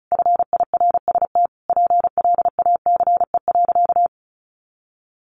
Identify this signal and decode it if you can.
Morse code, heard as a keyed tone, about 34 wpm, 700 Hz. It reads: FIRST PLACE.
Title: Additional Easy Phrases 35wpm